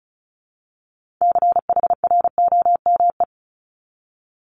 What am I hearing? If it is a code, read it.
Morse code, heard as a keyed tone, about 36 wpm, 700 Hz. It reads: CHROME
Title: Multi-letter Phonemes Repeated 3x 35wpm